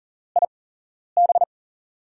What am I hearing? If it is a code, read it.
Morse code, heard as a keyed tone, about 43 wpm, spaced as for 11 wpm, 700 Hz. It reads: IB
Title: USA Call Signs - 6 Characters - 8x Character Spacing - 40wpm